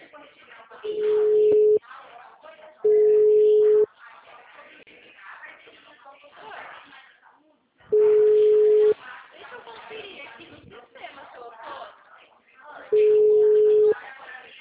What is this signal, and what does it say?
Ringback tone (Europe, 425 Hz), with speech.